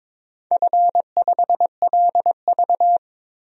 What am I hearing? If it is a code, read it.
Morse code, heard as a keyed tone, about 22 wpm, 700 Hz. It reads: F5LV
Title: DXCC-100 Call Signs - 4 Characters - 22wpm